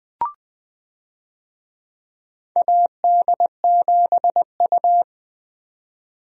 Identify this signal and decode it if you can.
Morse code, heard as a keyed tone, about 20 wpm, 700 Hz. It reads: AD7U
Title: USA Call Signs - 4 Characters - Repeated 3x - 20wpm